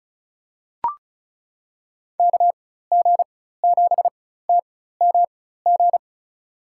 Morse code, heard as a keyed tone, about 36 wpm, spaced as for 16 wpm, 700 Hz. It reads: KG7TMG
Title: USA Call Signs - 6 Characters - 4x Character Spacing - 35wpm